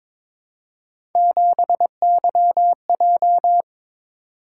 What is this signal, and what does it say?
Morse code, heard as a keyed tone, about 26 wpm, 700 Hz. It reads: 7YJ
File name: Speed-Racing - Three Letter-Number Combinations - 22wpm